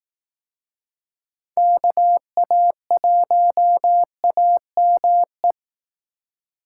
Morse code, heard as a keyed tone, about 18 wpm, 700 Hz. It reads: KA1AME
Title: Speed Racing - USA 2x3 Call Signs 15wpm